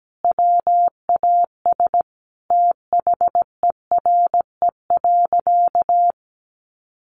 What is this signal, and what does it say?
Morse code, heard as a keyed tone, about 17 wpm, 700 Hz. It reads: WAS THERE.
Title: Sentences from Top 1400 Words 17wpm